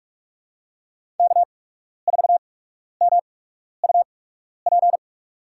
Morse code, heard as a keyed tone, about 44 wpm, spaced as for 12 wpm, 690 Hz. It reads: K4MUP
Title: USA Call Signs - 5 Characters - 8x Character Spacing - 45wpm